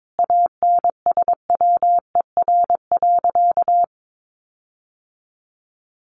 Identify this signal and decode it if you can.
Morse code, heard as a keyed tone, about 22 wpm, 700 Hz. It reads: ANSWER.
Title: Sentences from Top 1500 Words 22wpm